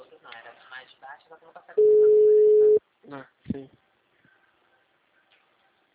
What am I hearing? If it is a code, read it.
Ringback tone (Europe, 425 Hz), with speech.